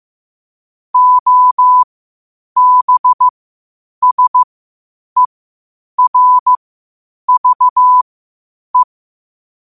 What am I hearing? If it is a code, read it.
Morse code, heard as a keyed tone, about 15 wpm, spaced as for 8 wpm, 1000 Hz. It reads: OBSERVE
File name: qotdfiles / 8wpm